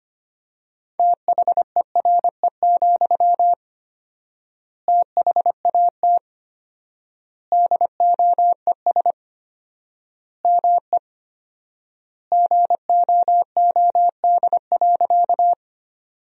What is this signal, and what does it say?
Morse code, heard as a keyed tone, about 25 wpm, 700 Hz. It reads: THERE, THAT DOES ME GOOD.
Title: 4x Word-Spacing - Sentences from Top 200 Words - 25wpm